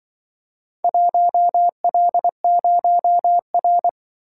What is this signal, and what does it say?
Morse code, heard as a keyed tone, about 24 wpm, 700 Hz. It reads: 1L0R
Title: Speed-Racing - Four Letter-Number Combinations - 20wpm